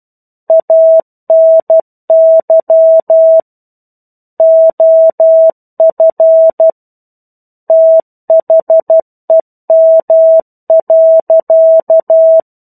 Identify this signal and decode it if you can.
Morse code, heard as a keyed tone, about 12 wpm, 650 Hz. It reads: ANY OF THEM.
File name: Chapter-01 12 WPM